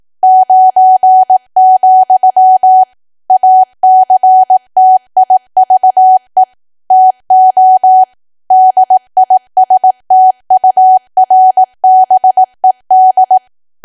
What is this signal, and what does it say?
Morse code, heard as a keyed tone, about 18 wpm, 750 Hz. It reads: 9, ACTIVE TO DISTURBED
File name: cwa / W1AW-Level2 / 18WPM